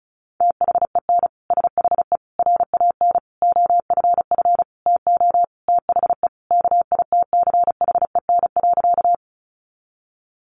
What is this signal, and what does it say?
Morse code, heard as a keyed tone, about 34 wpm, 700 Hz. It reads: THEN SHE RAN OFF TO THE KITCHEN.
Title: Sentences from Top 1400 Words 35wpm